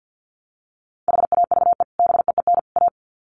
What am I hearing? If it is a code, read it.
Morse code, heard as a keyed tone, about 50 wpm, 700 Hz. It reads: HAVE BEEN A
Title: Sets of 3 Words 50wpm